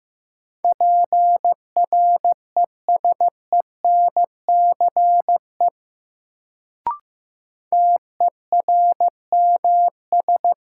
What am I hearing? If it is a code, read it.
Morse code, heard as a keyed tone, about 15 wpm, 700 Hz. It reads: PRESENCE TERMS
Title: Sets of 3 Words - New Word Drill - 15wpm